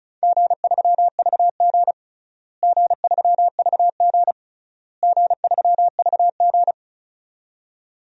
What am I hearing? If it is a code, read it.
Morse code, heard as a keyed tone, about 34 wpm, 700 Hz. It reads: G3VG G3VG G3VG
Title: DXCC-100 Call Signs - 4 Characters - Repeated 3x - 35wpm